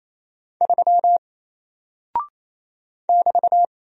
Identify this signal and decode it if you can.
Morse code, heard as a keyed tone, about 28 wpm, 700 Hz. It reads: 3 <BT>
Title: Single Character (Letter:Number:Punctuation:Prosign) Alt 28wpm